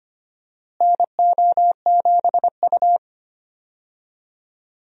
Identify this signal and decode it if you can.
Morse code, heard as a keyed tone, about 25 wpm, 700 Hz. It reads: NO7U
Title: USA Call Signs - 4 Characters - Repeated 2x - 25wpm